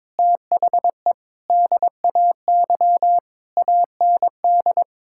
Morse code, heard as a keyed tone, about 22 wpm, 700 Hz. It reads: THE DAY AND
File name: practice / 22wpm / Sentences with 8-Word Phrases - 22wpm